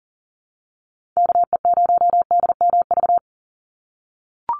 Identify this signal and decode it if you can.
Morse code, heard as a keyed tone, about 40 wpm, 700 Hz. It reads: KE0DMV
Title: USA Call Signs 40wpm